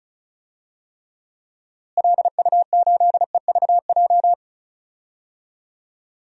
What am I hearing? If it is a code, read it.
Morse code, heard as a keyed tone, about 34 wpm, 700 Hz. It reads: LU8EVJ
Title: DXCC-100 Call Signs - 6 Characters - Repeated 2x - 35wpm